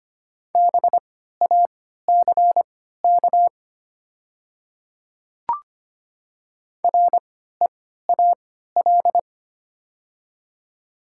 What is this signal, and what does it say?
Morse code, heard as a keyed tone, about 25 wpm, spaced as for 14 wpm, 700 Hz. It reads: BACK REAL
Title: 3x Character Spacing - Top 300 Words - 25wpm